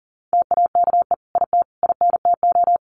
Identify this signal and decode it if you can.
Morse code, heard as a keyed tone, about 40 wpm, 700 Hz. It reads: TAKE IT INTO
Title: Sentences from Top 1500 Words 40wpm